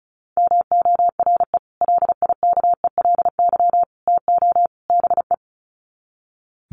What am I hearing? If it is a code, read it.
Morse code, heard as a keyed tone, about 34 wpm, 700 Hz. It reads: MORE LIKELY TO BE
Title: Sets of 4 Words 35wpm